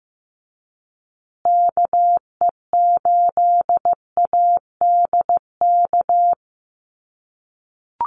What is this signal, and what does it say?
Morse code, heard as a keyed tone, about 15 wpm, 700 Hz. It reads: KE8ADK
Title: USA Call Signs Repeated 2x 15wpm